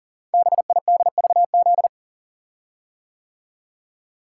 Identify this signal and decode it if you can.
Morse code, heard as a keyed tone, about 38 wpm, 690 Hz. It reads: BIDVZ
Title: Code Groups - 5 Letter:Numbers - 40wpm